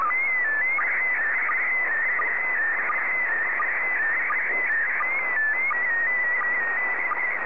SSTV (slow-scan television), PD 90.
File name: sstv / PD90-ZL2TJM